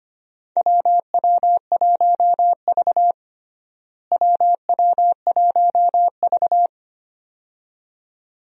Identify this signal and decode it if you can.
Morse code, heard as a keyed tone, about 25 wpm, 700 Hz. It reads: WW1V WW1V
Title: USA Call Signs - 4 Characters - Repeated 2x - 25wpm